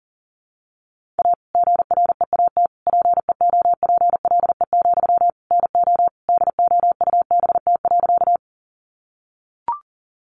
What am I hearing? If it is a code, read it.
Morse code, heard as a keyed tone, about 40 wpm, 700 Hz. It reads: A GREAT PEOPLE, NO DOUBT.
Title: Sentences from Top 1500 Words 40wpm